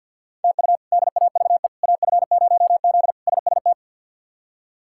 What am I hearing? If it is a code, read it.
Morse code, heard as a keyed tone, about 48 wpm, 690 Hz. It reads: TU DAVE AF0Z SST
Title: K1USN SST QSO Practice - 50wpm